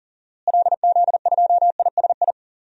Morse code, heard as a keyed tone, about 40 wpm, 700 Hz. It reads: LZ2ISI
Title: DXCC-100 Call Signs - 6 Characters - Repeated 3x - 40wpm